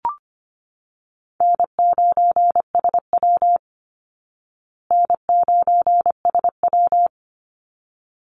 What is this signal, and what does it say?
Morse code, heard as a keyed tone, about 25 wpm, 700 Hz. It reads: N9SW N9SW
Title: Easy Call Signs Repeated 2x - 25wpm